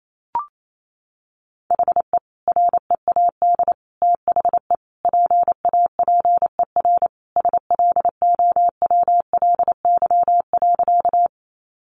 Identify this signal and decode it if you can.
Morse code, heard as a keyed tone, about 28 wpm, 700 Hz. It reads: HE READ THE PAPER SLOWLY.
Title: Sentences from Top 1100 Words 28wpm